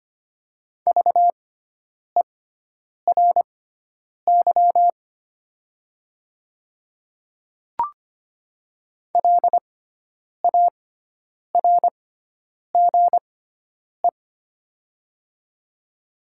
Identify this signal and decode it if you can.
Morse code, heard as a keyed tone, about 25 wpm, spaced as for 9 wpm, 700 Hz. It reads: VERY LARGE
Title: 6x Character Spacing - Top 300 Words - 25wpm